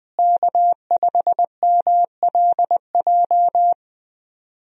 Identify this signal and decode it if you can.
Morse code, heard as a keyed tone, about 20 wpm, 700 Hz. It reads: K5MLJ
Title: USA Call Signs - 5 Characters - Repeated 2x - 20wpm